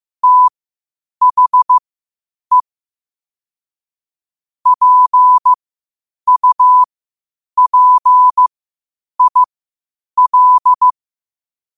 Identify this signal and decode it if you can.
Morse code, heard as a keyed tone, about 15 wpm, spaced as for 8 wpm, 1000 Hz. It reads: THE PUPIL
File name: qotdfiles / 8wpm